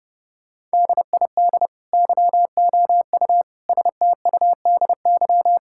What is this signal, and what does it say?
Morse code, heard as a keyed tone, about 30 wpm, 700 Hz. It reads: DID YOU STUDY
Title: Sentences from Top 1000 Words - Review Part 6 - 30wpm